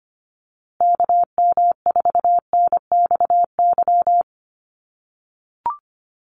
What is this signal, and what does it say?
Morse code, heard as a keyed tone, about 25 wpm, 700 Hz. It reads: KM4NXY
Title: USA Call Signs Repeated 2x 25wpm